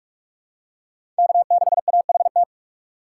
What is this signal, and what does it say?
Morse code, heard as a keyed tone, about 46 wpm, 700 Hz. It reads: K6AHT
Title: USA Call Signs - 5 Characters - 6x Character Spacing - 45wpm